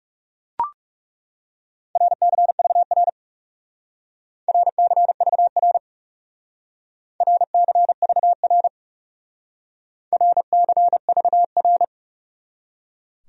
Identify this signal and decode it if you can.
Morse code, heard as a keyed tone, about 39 wpm, 700 Hz. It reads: RCVR RCVR RCVR RCVR